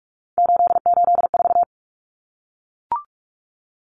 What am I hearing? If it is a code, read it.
Morse code, heard as a keyed tone, about 44 wpm, 700 Hz. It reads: 884
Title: Common Cut Numbers - 3-Digit 45wpm